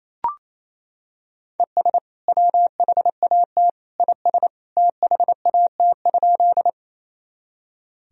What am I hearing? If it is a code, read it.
Morse code, heard as a keyed tone, about 28 wpm, 700 Hz. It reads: ES WHAT IS THAT?
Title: Sentences from Top 100 Words and Abbreviations 28wpm